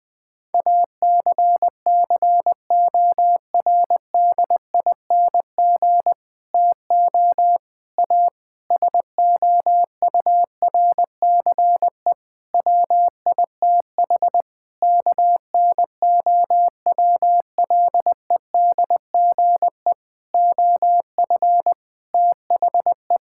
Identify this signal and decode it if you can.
Morse code, heard as a keyed tone, about 21 wpm, 700 Hz. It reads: ACCORDING TO A SOURCE WITH KNOWLEDGE OF THE
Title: Sets of 9 Words - 20wpm